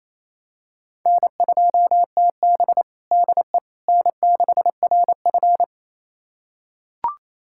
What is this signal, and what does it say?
Morse code, heard as a keyed tone, about 28 wpm, 700 Hz. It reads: N2TB DE N6RF
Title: USA Callsign DE Callsign - 4 Characters - 28wpm